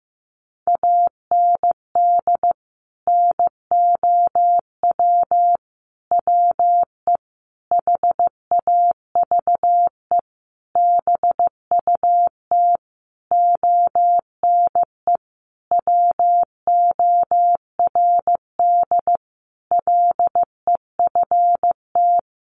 Morse code, heard as a keyed tone, about 15 wpm, 700 Hz. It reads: AND NOW WE HAVE BUT ONE WORD LEFT
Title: Sentences from Top 1500 Words 15wpm